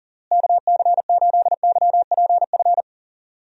Morse code, heard as a keyed tone, about 40 wpm, 700 Hz. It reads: KC8YPF
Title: USA Call Signs - 6 Characters - Repeated 3x - 40wpm